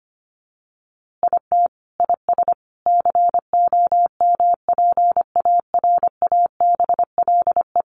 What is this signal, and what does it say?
Morse code, heard as a keyed tone, about 25 wpm, 700 Hz. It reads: IT IS COMPARABLE
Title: A Sound Of Thunder 25wpm